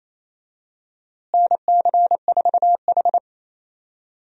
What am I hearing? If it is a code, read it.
Morse code, heard as a keyed tone, about 27 wpm, 700 Hz. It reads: NC4H
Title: USA Call Signs - 4 Characters - Repeated 3x - 28wpm